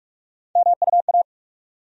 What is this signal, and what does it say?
Morse code, heard as a keyed tone, about 46 wpm, 700 Hz. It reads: MUA
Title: USA Call Sign Postfixes - 3 Letters 45wpm